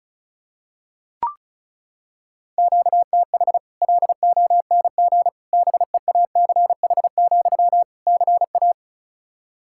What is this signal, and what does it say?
Morse code, heard as a keyed tone, about 34 wpm, 700 Hz. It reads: QTH LONG BEACH, CA
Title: Practice QSO - Exchange 1 - Beginning - 35wpm